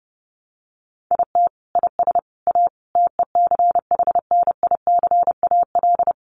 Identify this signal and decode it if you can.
Morse code, heard as a keyed tone, about 30 wpm, 700 Hz. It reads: IT IS A TECHNICAL
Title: Sentences from Top 1100 Words 30wpm